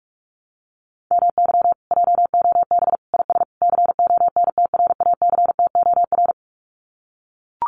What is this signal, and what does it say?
Morse code, heard as a keyed tone, about 44 wpm, 700 Hz. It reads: MY JOB IS CONTRACTOR
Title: Common QSO Phrases Encore 45wpm